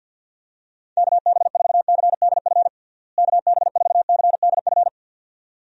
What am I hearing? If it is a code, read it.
Morse code, heard as a keyed tone, about 46 wpm, 700 Hz. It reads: KB4CDF KB4CDF
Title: USA Call Signs - 6 Characters - Repeated 2x - 50wpm